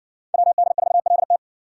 Morse code, heard as a keyed tone, about 56 wpm, 700 Hz. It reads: WD4LT